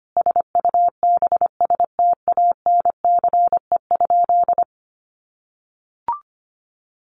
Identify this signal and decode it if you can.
Morse code, heard as a keyed tone, about 25 wpm, 700 Hz. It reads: SUBSTANCE?
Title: Sentences from Top 1500 Words 25wpm